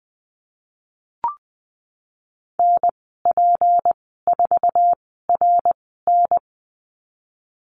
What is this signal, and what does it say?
Morse code, heard as a keyed tone, about 20 wpm, spaced as for 15 wpm, 700 Hz. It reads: NP4RN
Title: 167 Lesson 27g - Number 7 - Call Signs 20wpm